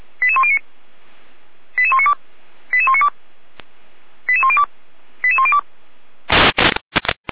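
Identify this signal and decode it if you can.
Five-tone selective call (ZVEI), heard as quick runs of five tones, tones of about 70 ms. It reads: 88109 88192 88192 88102 88102
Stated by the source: ZVEI five-tone selective call received on SDR.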